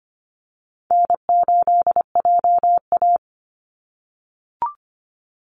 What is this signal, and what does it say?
Morse code, heard as a keyed tone, about 25 wpm, 700 Hz. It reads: N8JA
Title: Call Signs - 25wpm